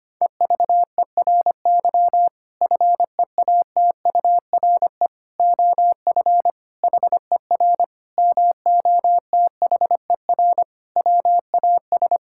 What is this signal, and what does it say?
Morse code, heard as a keyed tone, about 25 wpm, 700 Hz. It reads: EVERY FEATURE OF HER MOTHER WAS
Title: Sentences from Top 1000 Words - Review Part 5 - 25wpm